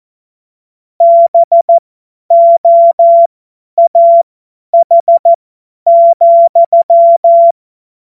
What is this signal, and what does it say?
Morse code, heard as a keyed tone, about 14 wpm, spaced as for 10 wpm, 680 Hz. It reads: BOAH,
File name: Morsekurs / Witze / Witz 07. Dezember 2021, 50 F70